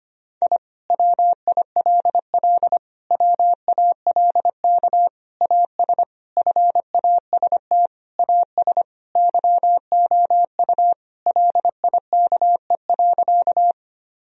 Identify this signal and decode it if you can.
Morse code, heard as a keyed tone, about 25 wpm, 700 Hz. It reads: I WILL WALK AS FAST AS YOU LIKE.
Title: Sentences from Top 1000 Words - Review Part 2 - 25wpm